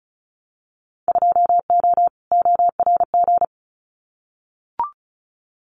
Morse code, heard as a keyed tone, about 34 wpm, 700 Hz. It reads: 2O ORG
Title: Winter Field Day Exchanges 35wpm